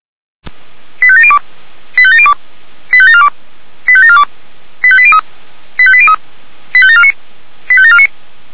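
Five-tone selective call (ZVEI), heard as quick runs of five tones, tones of about 70 ms. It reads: 85702 85702 85742 85742 85803 85803 85749 85749
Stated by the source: ZVEI five-tone selective call received by SDR.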